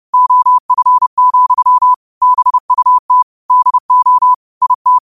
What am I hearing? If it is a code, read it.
Morse code, heard as a keyed tone, about 30 wpm, 1000 Hz. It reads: OF, BUT DO IT
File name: qotdfiles / 30wpm